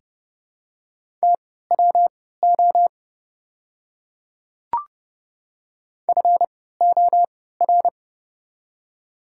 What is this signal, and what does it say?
Morse code, heard as a keyed tone, about 30 wpm, spaced as for 17 wpm, 700 Hz. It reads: TWO FOR
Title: 3x Character Spacing - Top 100 Words - 30wpm